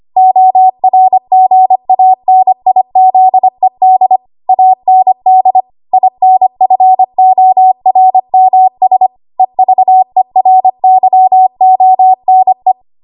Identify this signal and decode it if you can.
Morse code, heard as a keyed tone, about 25 wpm, 750 Hz. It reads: ORGANIZED AND INFORMS EVERYONE
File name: Morse / 25WPM